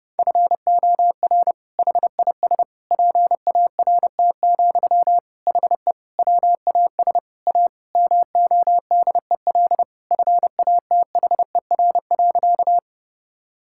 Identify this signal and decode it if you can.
Morse code, heard as a keyed tone, about 32 wpm, 700 Hz. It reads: FOR HIS PART, HE WAS A MODEL FATHER.
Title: Sentences from Top 1000 Words - Review Part 5 - 30wpm